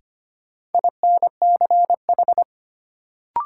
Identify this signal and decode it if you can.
Morse code, heard as a keyed tone, about 25 wpm, 700 Hz. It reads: INCH
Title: Sentences from Top 1000 Words - Review Part 3 - New Word Drill - 25wpm